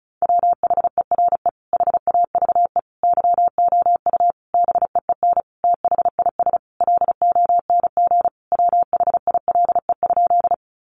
Morse code, heard as a keyed tone, about 34 wpm, 700 Hz. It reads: WHERE HAVE YOU BEEN THIS LONG WHILE?
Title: Sentences from Top 300 Words 35wpm